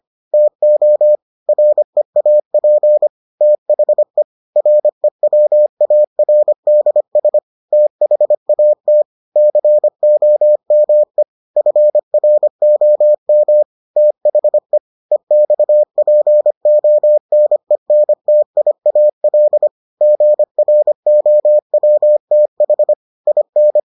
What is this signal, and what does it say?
Morse code, heard as a keyed tone, about 25 wpm, 600 Hz. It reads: TO REAP THE REWARDS THAT COME FROM THE EXPONENTIAL GROWTH IN